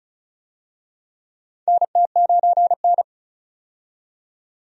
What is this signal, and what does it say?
Morse code, heard as a keyed tone, about 34 wpm, 700 Hz. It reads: NT9N
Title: USA Call Signs - 4 Characters - 35wpm